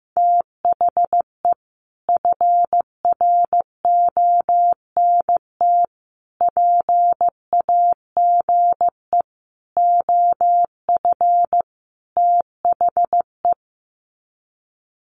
Morse code, heard as a keyed tone, about 15 wpm, 700 Hz. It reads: THE FRONT PAGE OF THE
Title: Sets of 5 Words Encore 15wpm